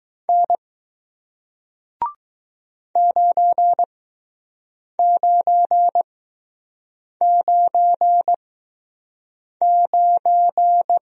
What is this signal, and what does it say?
Morse code, heard as a keyed tone, about 20 wpm, spaced as for 6 wpm, 700 Hz. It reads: N 9999